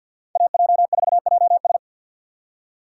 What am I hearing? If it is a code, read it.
Morse code, heard as a keyed tone, about 51 wpm, 700 Hz. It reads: AJ4JS
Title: USA Call Signs - 5 Characters - 4x Character Spacing - 50wpm